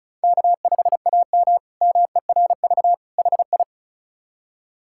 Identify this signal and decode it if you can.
Morse code, heard as a keyed tone, about 36 wpm, 700 Hz. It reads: K5AM MERV HI
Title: K1USN SST QSO Practice - 35wpm